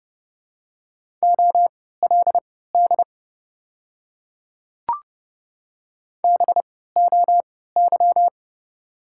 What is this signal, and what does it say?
Morse code, heard as a keyed tone, about 31 wpm, spaced as for 17 wpm, 690 Hz. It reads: OLD BOY
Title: 3x Character Spacing - Top 200 Words - 30wpm